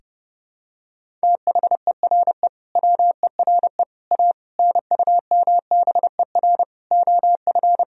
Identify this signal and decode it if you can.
Morse code, heard as a keyed tone, about 32 wpm, 700 Hz. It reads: THERE WERE A NUMBER OF
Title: Sentences from Top 1000 Words - Review Part 1 - 30wpm